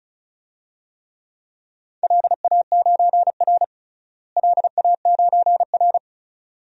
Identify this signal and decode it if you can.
Morse code, heard as a keyed tone, about 34 wpm, 700 Hz. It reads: LA9R LA9R
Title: DXCC-100 Call Signs - 4 Characters - Repeated 2x - 35wpm